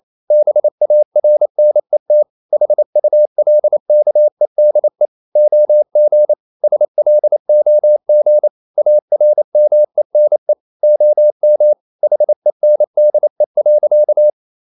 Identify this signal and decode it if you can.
Morse code, heard as a keyed tone, about 28 wpm, 600 Hz. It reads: BARNET HULKEDE OG SLOG ARMENE OM HENDE.
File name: Ferie QRQ 28wpm